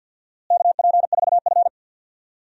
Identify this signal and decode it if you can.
Morse code, heard as a keyed tone, about 53 wpm, 700 Hz. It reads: KP4F
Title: DXCC-100 Call Signs - 4 Characters - Repeated 2x - 50wpm